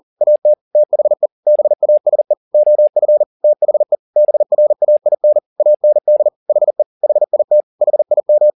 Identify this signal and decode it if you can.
Morse code, heard as a keyed tone, about 40 wpm, 600 Hz. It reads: AT THE BASE OF THE BRAIN AND HE HIT HIM
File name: the old man and the sea 2 40wpm